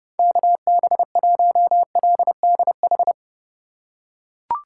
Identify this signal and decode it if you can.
Morse code, heard as a keyed tone, about 27 wpm, 700 Hz. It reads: KB1LDH